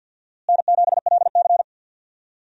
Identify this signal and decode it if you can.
Morse code, heard as a keyed tone, about 52 wpm, 700 Hz. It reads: N7LC
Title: USA Call Signs - 4 Characters - Repeated 2x - 50wpm